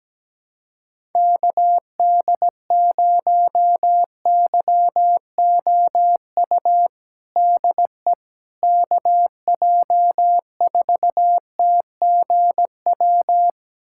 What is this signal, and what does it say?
Morse code, heard as a keyed tone, about 17 wpm, 700 Hz. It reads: KD0YOU DE KJ4TGW
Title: USA Callsign DE Callsign - 6 Characters - 17wpm